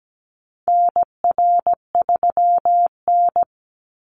Morse code, heard as a keyed tone, about 17 wpm, 700 Hz. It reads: NR3N
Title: Call Signs - 17wpm